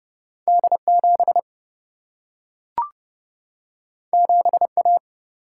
Morse code, heard as a keyed tone, about 30 wpm, 700 Hz. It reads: D7 7A
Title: Single-Letter Single-Number 30wpm